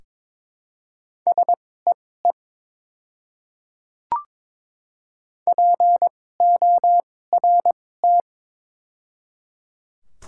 Morse code, heard as a keyed tone, about 22 wpm, spaced as for 16 wpm, 700 Hz. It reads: SEE PORT
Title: 2x Character Spacing - Top 300 Words - 22wpm